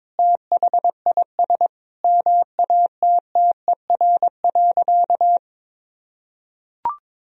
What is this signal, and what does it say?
Morse code, heard as a keyed tone, about 22 wpm, 700 Hz. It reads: THIS MATTER.
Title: Sentences from Top 1000 Words - Review Part 2 - 22wpm